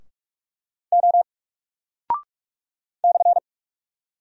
Morse code, heard as a keyed tone, about 44 wpm, 700 Hz. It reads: O /
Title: Single Character (Letter:Number:Punctuation:Prosign) 45wpm